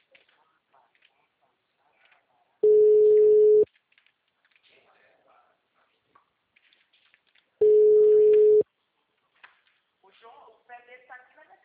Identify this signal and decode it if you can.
Ringback tone (Europe, 425 Hz), with speech.